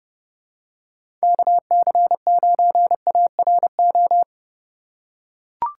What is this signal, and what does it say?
Morse code, heard as a keyed tone, about 31 wpm, 700 Hz. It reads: KC9ARO
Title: USA Call Signs - 6 Characters - 2x Character Spacing - 30wpm